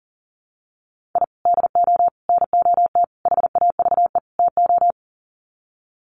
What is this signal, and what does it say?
Morse code, heard as a keyed tone, about 40 wpm, 700 Hz. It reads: I DO NOT HAVE TO
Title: Sets of 5 Words 40wpm